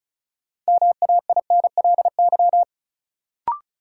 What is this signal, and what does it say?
Morse code, heard as a keyed tone, about 36 wpm, 700 Hz. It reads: MAINLY
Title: Sentences from Top 1400 Words - New Word Drill Alt - 35wpm